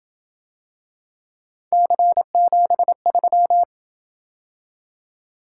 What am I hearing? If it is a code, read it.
Morse code, heard as a keyed tone, about 27 wpm, 700 Hz. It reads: C73